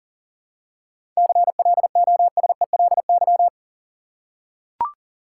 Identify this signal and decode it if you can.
Morse code, heard as a keyed tone, about 40 wpm, 700 Hz. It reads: CLOSELY
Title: Seven Letter Words - 40wpm